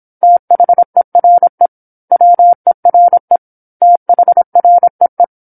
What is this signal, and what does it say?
Morse code, heard as a keyed tone, about 27 wpm, 700 Hz. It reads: THERE WERE THREE
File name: 08-Chapter-08 26 WPM